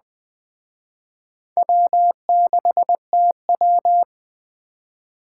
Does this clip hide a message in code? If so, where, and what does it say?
Yes: Morse code from 1.6 to 4.0 s, heard as a keyed tone, about 22 wpm, 700 Hz. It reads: W6TW